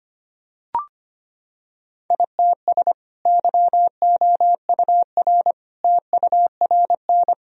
Morse code, heard as a keyed tone, about 25 wpm, 700 Hz. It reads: ITS YOUR TURN
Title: Sentences from Top 1000 Words - Review Part 1 - 25wpm